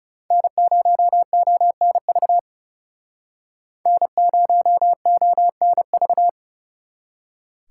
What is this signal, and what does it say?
Morse code, heard as a keyed tone, about 37 wpm, 700 Hz. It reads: N0ONV N0ONV
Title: Speed Racing - USA 1x3 Call Signs 30wpm